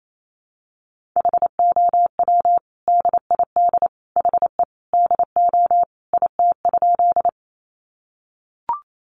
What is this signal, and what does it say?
Morse code, heard as a keyed tone, about 28 wpm, 700 Hz. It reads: HOW DID HE DO IT?
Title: Sentences from Top 100 Words 28wpm